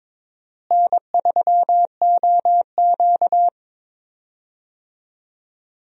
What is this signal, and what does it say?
Morse code, heard as a keyed tone, about 22 wpm, 700 Hz. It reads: N3OQ
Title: USA Call Signs - 4 Characters - Repeated 2x - 22wpm